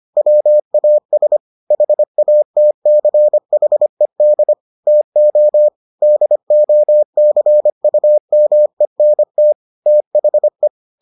Morse code, heard as a keyed tone, about 25 wpm, 600 Hz. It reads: WAS HATCHED TO DOCUMENT THE